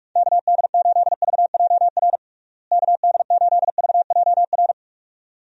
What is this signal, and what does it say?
Morse code, heard as a keyed tone, about 41 wpm, 700 Hz. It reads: KD8VJR KD8VJR
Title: USA Call Signs - 6 Characters - Repeated 2x - 45wpm